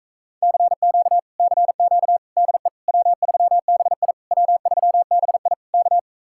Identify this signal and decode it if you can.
Morse code, heard as a keyed tone, about 37 wpm, 700 Hz. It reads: CQ CQ DE W3BI W3BI K